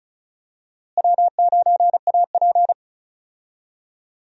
Morse code, heard as a keyed tone, about 36 wpm, 700 Hz. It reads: W9AP
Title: USA Call Signs - 4 Characters - Repeated 3x - 35wpm